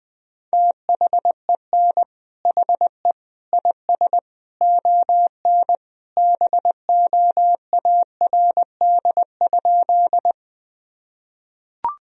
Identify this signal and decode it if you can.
Morse code, heard as a keyed tone, about 20 wpm, 700 Hz. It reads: THEN HE IS ON BOARD?
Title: Sentences from Top 1000 Words - Review Part 4 - 20wpm